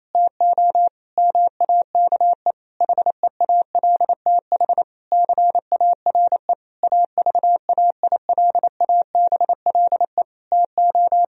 Morse code, heard as a keyed tone, about 28 wpm, 700 Hz. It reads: TO MAKE HEALTH CARE AVAILABLE TO
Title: Sentences from Top 1000 Words - Review Part 3 - 28wpm